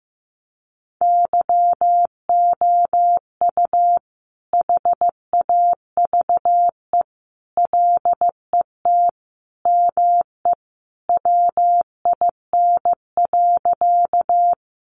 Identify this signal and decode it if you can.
Morse code, heard as a keyed tone, about 15 wpm, 700 Hz. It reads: YOU HAVE LET ME WIN.
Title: Sentences from Top 1500 Words 15wpm